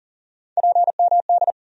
Morse code, heard as a keyed tone, about 40 wpm, 700 Hz. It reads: PMD
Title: USA Call Sign Postfixes - 3 Letters 40wpm